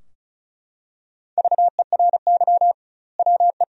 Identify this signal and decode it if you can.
Morse code, heard as a keyed tone, about 36 wpm, 700 Hz. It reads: VERY WE
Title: 2x Word-Spacing - Sentences from Top 500 Words - 35wpm